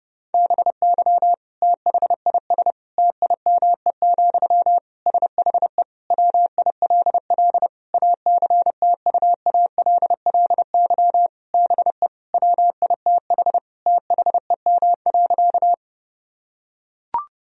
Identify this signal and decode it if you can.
Morse code, heard as a keyed tone, about 30 wpm, 700 Hz. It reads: BY THIS TIME, SHE WILL ACTUALLY BE WITH THEM.
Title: Sentences from Top 1000 Words - Review Part 4 - 30wpm